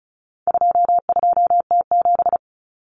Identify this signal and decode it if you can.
Morse code, heard as a keyed tone, about 34 wpm, 700 Hz. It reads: 22T7
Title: Common Cut Numbers - 4-Digit 35wpm